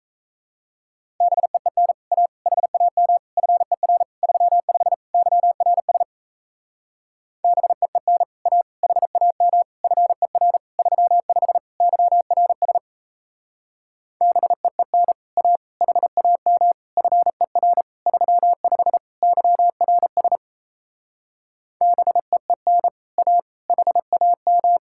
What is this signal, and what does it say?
Morse code, heard as a keyed tone, about 38 wpm, 700 Hz. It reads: BEEN A HAM FER 35 YRS BEEN A HAM FER 35 YRS BEEN A HAM FER 35 YRS BEEN A HAM